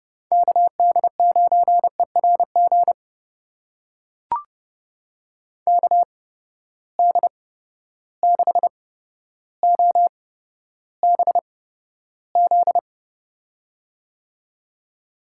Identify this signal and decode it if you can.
Morse code, heard as a keyed tone, about 32 wpm, spaced as for 13 wpm, 700 Hz. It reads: KD9ERG KD6OBZ
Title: USA Call Signs - 6 Characters - 8x Character Spacing - 30wpm